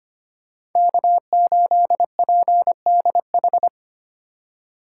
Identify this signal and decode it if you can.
Morse code, heard as a keyed tone, about 25 wpm, 700 Hz. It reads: K8PDH
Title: USA Call Signs - 5 Characters - 3x Character Spacing - 25wpm